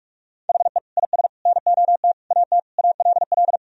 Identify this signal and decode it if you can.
Morse code, heard as a keyed tone, about 42 wpm, 700 Hz. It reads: HE IS NOT AT ALL
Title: Sentences from Top 1000 Words - Review Part 4 - 45wpm